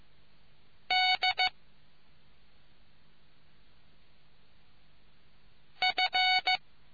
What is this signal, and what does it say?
Morse code, heard as a keyed tone, about 15 wpm, 740 Hz. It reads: D F